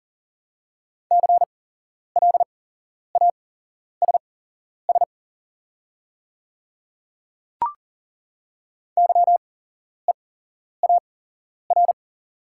Morse code, heard as a keyed tone, about 40 wpm, spaced as for 11 wpm, 700 Hz. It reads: CLASS YEAR
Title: 8x Character Spacing - Top 500 Words - 40wpm